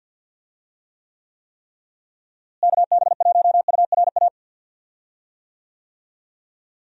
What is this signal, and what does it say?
Morse code, heard as a keyed tone, about 52 wpm, 700 Hz. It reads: KB1URA
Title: USA Call Signs - 6 Characters - Repeated 3x - 50wpm